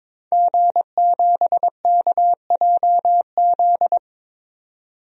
Morse code, heard as a keyed tone, about 22 wpm, 700 Hz. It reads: G7KJZ
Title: DXCC-100 Call Signs - 5 Characters - 22wpm